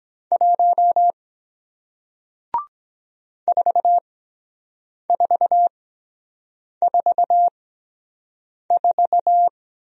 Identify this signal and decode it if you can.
Morse code, heard as a keyed tone, about 23 wpm, spaced as for 7 wpm, 700 Hz. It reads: 1 4444